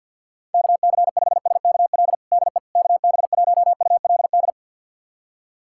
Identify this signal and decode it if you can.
Morse code, heard as a keyed tone, about 51 wpm, 690 Hz. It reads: KC5SKL DE KB1ULD
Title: USA Callsign DE Callsign - 6 Characters - 50wpm